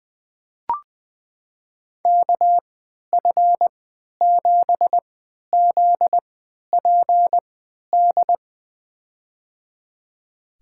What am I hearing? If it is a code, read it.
Morse code, heard as a keyed tone, about 20 wpm, spaced as for 11 wpm, 700 Hz. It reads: KF7ZPD
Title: USA Call Signs - 6 Characters - 3x Character Spacing - 20wpm